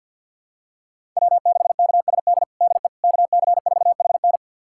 Morse code, heard as a keyed tone, about 46 wpm, 700 Hz. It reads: W6KSD DE KC4HN
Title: USA Callsign DE Callsign - 5 Characters - 50wpm